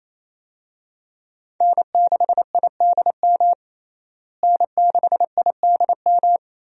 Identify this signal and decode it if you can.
Morse code, heard as a keyed tone, about 28 wpm, 700 Hz. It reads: N6IDM N6IDM
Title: USA Call Signs - 5 Characters - Repeated 3x - 28wpm